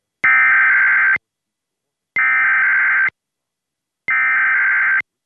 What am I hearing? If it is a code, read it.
Emergency Alert System (SAME) tones (header bursts), heard as harsh buzzing bursts. It is heard from 0.2 to 5.0 s.